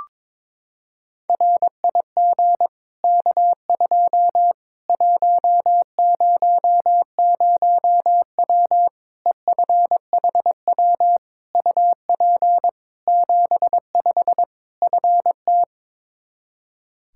Morse code, heard as a keyed tone, about 22 wpm, 700 Hz. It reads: RIG K2 100W EFHW UP 75 FT
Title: Practice QSO - Exchange 2 - Info - 22wpm